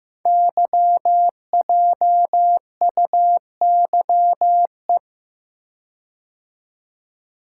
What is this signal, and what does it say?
Morse code, heard as a keyed tone, about 15 wpm, 700 Hz. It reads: YJUYE
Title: Code Groups - 5 Letters - 15wpm